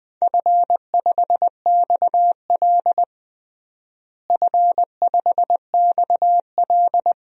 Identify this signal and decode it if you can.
Morse code, heard as a keyed tone, about 20 wpm, 700 Hz. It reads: F5XL F5XL
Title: DXCC-100 Call Signs - 4 Characters - Repeated 3x - 20wpm